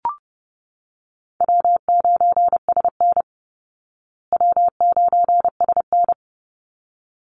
Morse code, heard as a keyed tone, about 30 wpm, 700 Hz. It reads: W9SN W9SN
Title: Call Signs Repeated 3x - 30wpm